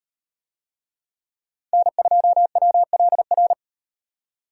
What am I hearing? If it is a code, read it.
Morse code, heard as a keyed tone, about 38 wpm, 700 Hz. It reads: N2WLR